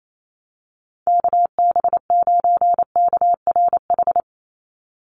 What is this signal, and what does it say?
Morse code, heard as a keyed tone, about 28 wpm, 700 Hz. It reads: KB9KRH
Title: USA Call Signs 28wpm